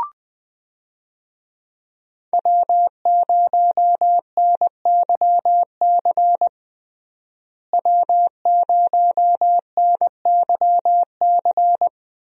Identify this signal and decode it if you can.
Morse code, heard as a keyed tone, about 20 wpm, 700 Hz. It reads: W0NYC W0NYC
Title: USA Call Signs - 5 Characters - Repeated 2x - 20wpm